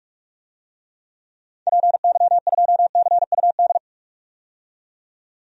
Morse code, heard as a keyed tone, about 46 wpm, 700 Hz. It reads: PY2CUD
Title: DXCC-100 Call Signs - 6 Characters - 45wpm